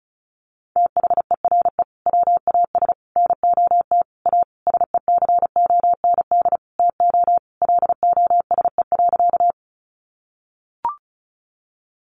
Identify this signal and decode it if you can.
Morse code, heard as a keyed tone, about 34 wpm, 700 Hz. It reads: THERE WAS NOT A SECOND TO LOSE.
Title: Sentences from Top 1100 Words 35wpm